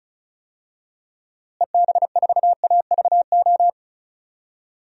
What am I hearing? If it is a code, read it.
Morse code, heard as a keyed tone, about 36 wpm, 700 Hz. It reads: EB4AVO
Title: DXCC-100 Call Signs - 6 Characters - Repeated 3x - 35wpm